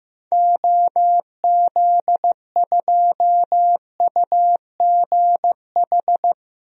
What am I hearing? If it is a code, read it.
Morse code, heard as a keyed tone, about 15 wpm, 700 Hz. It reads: OZ2UGH
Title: DXCC-100 Call Signs - 6 Characters - Repeated 3x - 15wpm